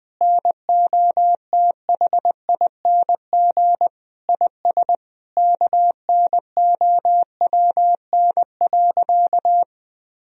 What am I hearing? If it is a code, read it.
Morse code, heard as a keyed tone, about 20 wpm, 700 Hz. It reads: NOTHING IS KNOWN.
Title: Sentences from Top 1000 Words - Review Part 4 - 20wpm